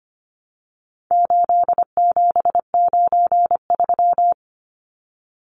Morse code, heard as a keyed tone, about 25 wpm, 700 Hz. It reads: 8793
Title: Numbers - 4-Digit 25wpm